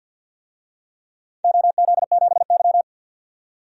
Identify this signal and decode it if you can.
Morse code, heard as a keyed tone, about 51 wpm, 700 Hz. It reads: OZ7Y
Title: DXCC-100 Call Signs - 4 Characters - 50wpm